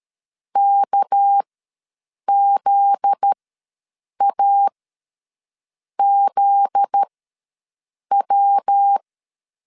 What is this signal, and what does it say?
Morse code, heard as a keyed tone, about 13 wpm, spaced as for 6 wpm, 800 Hz. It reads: KZAZW
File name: LEZIONI / 7 GF9Z / 7 GF9Z ws 12wpm 3321